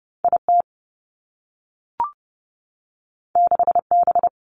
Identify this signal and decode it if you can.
Morse code, heard as a keyed tone, about 30 wpm, 700 Hz. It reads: I T 6 B
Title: Two Challenging Letter-Number Combinations 30wpm